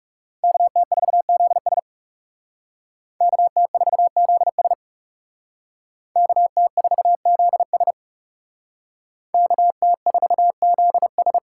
Morse code, heard as a keyed tone, about 36 wpm, 700 Hz. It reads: KT4ZS KT4ZS KT4ZS KT4ZS
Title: Speed Racing - USA 2x2 Call Signs 30wpm